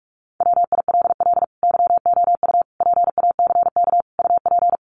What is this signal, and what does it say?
Morse code, heard as a keyed tone, about 44 wpm, 700 Hz. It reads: WILL YOU PACK UP
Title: Sentences from Top 1300 Words 45wpm